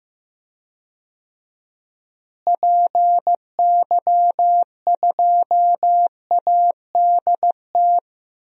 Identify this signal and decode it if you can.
Morse code, heard as a keyed tone, about 15 wpm, 700 Hz. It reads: PY2ADT
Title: DXCC-100 Call Signs - 6 Characters - Repeated 2x - 15wpm